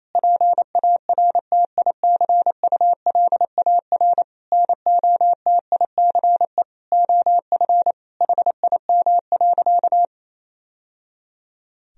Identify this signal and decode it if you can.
Morse code, heard as a keyed tone, about 28 wpm, 700 Hz. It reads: PARTICULAR NOTICE OF HIM.
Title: Sentences from Top 1000 Words - Review Part 4 - 28wpm